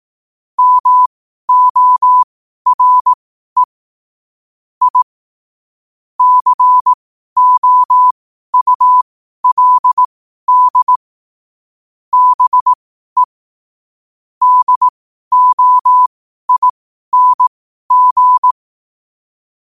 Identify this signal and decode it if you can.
Morse code, heard as a keyed tone, about 18 wpm, spaced as for 12 wpm, 1000 Hz. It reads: MORE I COULD BE DOING
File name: qotdfiles / 12wpm